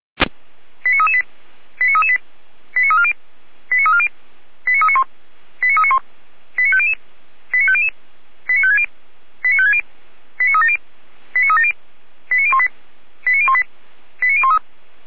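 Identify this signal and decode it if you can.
Five-tone selective call (ZVEI), heard as quick runs of five tones, tones of about 80 ms. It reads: 89338 89338 89340 89340 89391 89391 89500 89500 89560 89560 89370 89370 89018 89018 89013
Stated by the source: ZVEI five-tone selective call received on SDR.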